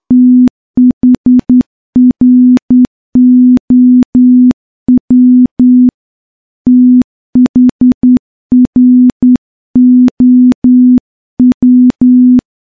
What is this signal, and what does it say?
Morse code, heard as a keyed tone, about 10 wpm, 250 Hz. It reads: THROW THROW
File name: morse-training-10WPM